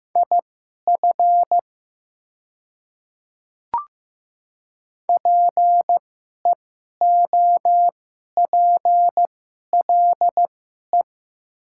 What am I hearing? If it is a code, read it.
Morse code, heard as a keyed tone, about 15 wpm, spaced as for 11 wpm, 690 Hz. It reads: IF PEOPLE
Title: 2x Character Spacing - Top 100 Words - 15wpm